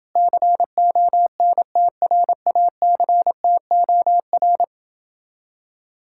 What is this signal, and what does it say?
Morse code, heard as a keyed tone, about 28 wpm, 700 Hz. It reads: CONTRACTOR